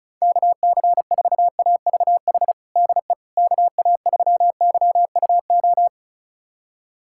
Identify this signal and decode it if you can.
Morse code, heard as a keyed tone, about 36 wpm, 700 Hz. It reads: KC4AVH DE KA3YUO
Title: USA Callsign DE Callsign - 6 Characters - 35wpm